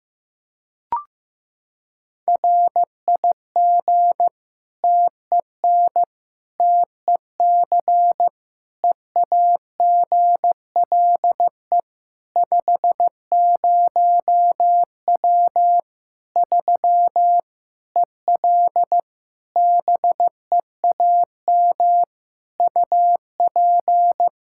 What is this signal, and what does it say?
Morse code, heard as a keyed tone, about 16 wpm, 700 Hz. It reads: RIG TEN TEC EAGLE 50W 3 EL BEAM UP
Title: Practice QSO - Exchange 2 - Info - 15wpm